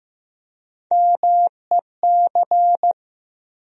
Morse code, heard as a keyed tone, about 15 wpm, 700 Hz. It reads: MEC
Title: USA Call Sign Postfixes - 3 Letters 15wpm